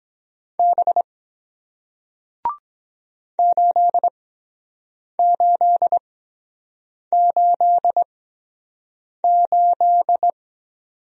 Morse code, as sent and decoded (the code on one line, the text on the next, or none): -... / ---.. ---.. ---.. ---..
B 8888